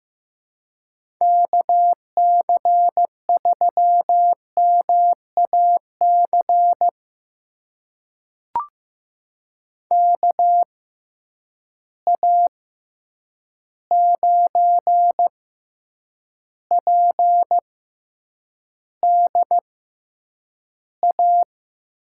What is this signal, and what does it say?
Morse code, heard as a keyed tone, about 15 wpm, spaced as for 8 wpm, 700 Hz. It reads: KC3MAC KA9PDA
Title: USA Call Signs - 6 Characters - 6x Character Spacing - 15wpm